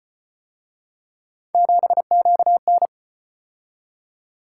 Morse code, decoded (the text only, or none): 7QN